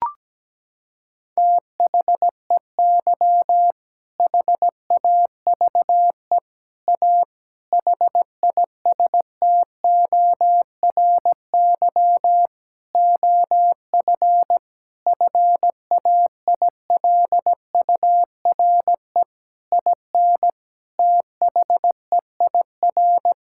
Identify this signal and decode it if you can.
Morse code, heard as a keyed tone, about 17 wpm, 700 Hz. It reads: THEY HAVE A HISTORY OF FAILURE IN THEIR
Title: Sentences from Top 1000 Words - Review Part 6 - 17wpm